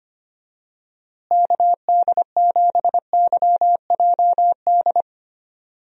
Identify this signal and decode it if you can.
Morse code, heard as a keyed tone, about 25 wpm, 700 Hz. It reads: KD7YJD
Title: USA Call Signs - 6 Characters - Repeated 2x - 25wpm